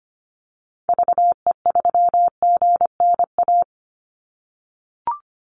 Morse code, heard as a keyed tone, about 25 wpm, 700 Hz. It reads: VE3GNA
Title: Call Signs - 25wpm